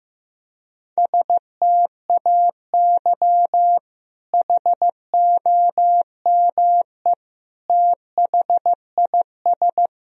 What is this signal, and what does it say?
Morse code, heard as a keyed tone, about 15 wpm, 700 Hz. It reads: STAY HOME THIS
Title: Sentences from Top 1000 Words - Review Part 1 - 15wpm